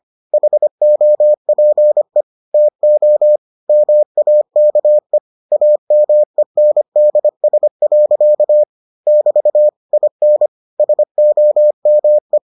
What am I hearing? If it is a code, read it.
Morse code, heard as a keyed tone, about 25 wpm, 600 Hz. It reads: HOPE TO MAKE AMENDS. <BT> IN SOME